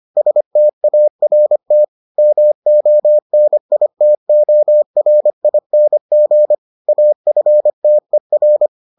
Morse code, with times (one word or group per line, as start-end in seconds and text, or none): 0.2-1.8 START
2.2-6.5 MONITORING
6.9-8.7 AFTER